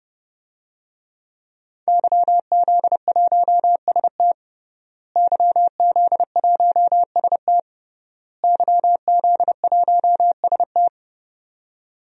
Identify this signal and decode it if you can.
Morse code, heard as a keyed tone, about 30 wpm, 700 Hz. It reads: YZ1ST YZ1ST YZ1ST
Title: DXCC-100 Call Signs - 5 Characters - Repeated 3x - 30wpm